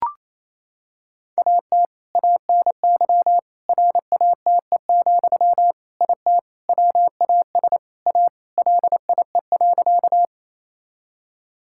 Morse code, heard as a keyed tone, about 28 wpm, 700 Hz. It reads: AT ANY RATE, IT WAS A LIE.
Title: Sentences from Top 1600 Words 28wpm